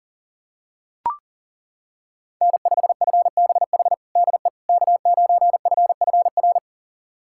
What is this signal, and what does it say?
Morse code, heard as a keyed tone, about 40 wpm, 700 Hz. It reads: N5FBH DE K9FFR
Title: USA Callsign DE Callsign - 5 Characters - 40wpm